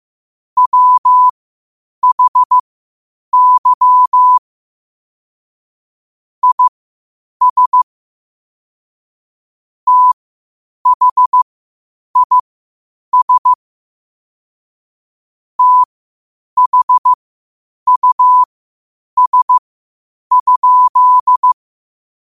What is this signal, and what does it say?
Morse code, heard as a keyed tone, about 15 wpm, spaced as for 8 wpm, 1000 Hz. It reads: WHY IS THIS THUS?
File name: qotdfiles / 8wpm